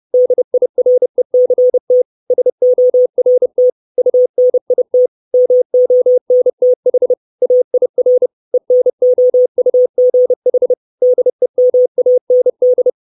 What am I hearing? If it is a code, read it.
Morse code, heard as a keyed tone, about 30 wpm, 500 Hz. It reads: DIRECT SORT UNIT MONTH AIR ENOUGH DEMAND
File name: 632 Words 30WPM